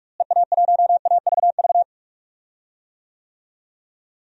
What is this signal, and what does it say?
Morse code, heard as a keyed tone, about 46 wpm, 700 Hz. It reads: EA1AVV
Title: DXCC-100 Call Signs - 6 Characters - Repeated 3x - 45wpm